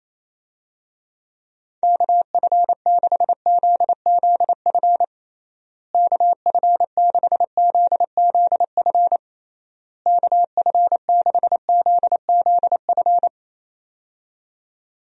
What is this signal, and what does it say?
Morse code, heard as a keyed tone, about 28 wpm, 700 Hz. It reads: KF6ZZF KF6ZZF KF6ZZF
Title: USA Call Signs - 6 Characters - Repeated 3x - 28wpm